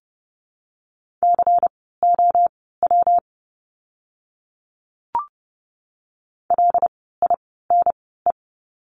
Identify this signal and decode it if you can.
Morse code, heard as a keyed tone, about 30 wpm, spaced as for 17 wpm, 700 Hz. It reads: COW LINE
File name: course / 30:10wpm / 115 Lesson 19f - Letter w - Words 30wpm